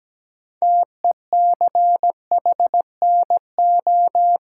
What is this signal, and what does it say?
Morse code, heard as a keyed tone, about 17 wpm, 700 Hz. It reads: TECHNO
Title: Sentences from Top 1000 Words - Review Part 5 - New Word Drill - 17wpm